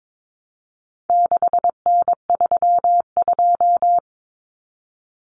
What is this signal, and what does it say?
Morse code, heard as a keyed tone, about 22 wpm, 700 Hz. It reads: 6N32
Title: Common Cut Numbers - 4-Digit 22wpm